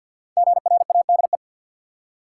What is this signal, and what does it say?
Morse code, heard as a keyed tone, about 50 wpm, 700 Hz. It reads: GRADE
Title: Five Letter Words - 50wpm